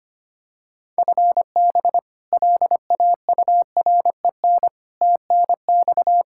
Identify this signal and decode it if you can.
Morse code, heard as a keyed tone, about 25 wpm, 700 Hz. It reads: FB LAUREN TNX
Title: K1USN SST QSO Practice 25wpm